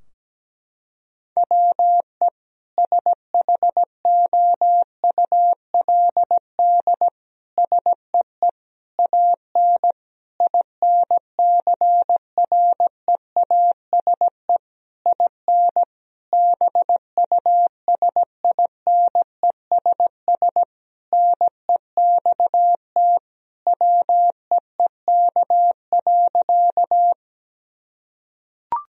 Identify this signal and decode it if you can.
Morse code, heard as a keyed tone, about 17 wpm, 700 Hz. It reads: WE SHOULD SEE AN INCREASE IN BUSINESS NEXT WEEK.
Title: Sentences from Top 1000 Words - Review Part 4 - 17wpm